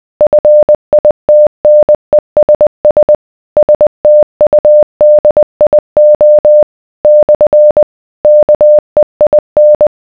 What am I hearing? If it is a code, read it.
Morse code, heard as a keyed tone, about 20 wpm, 600 Hz. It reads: FITNESS STUDIO / KEIN